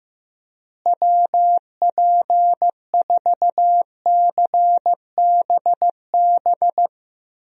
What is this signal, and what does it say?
Morse code, heard as a keyed tone, about 15 wpm, 700 Hz. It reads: WP4CBB
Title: USA Call Signs - 6 Characters - Repeated 3x - 15wpm